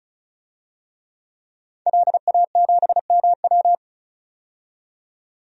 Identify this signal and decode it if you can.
Morse code, heard as a keyed tone, about 34 wpm, 700 Hz. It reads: LA7MW
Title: DXCC-100 Call Signs - 5 Characters - 35wpm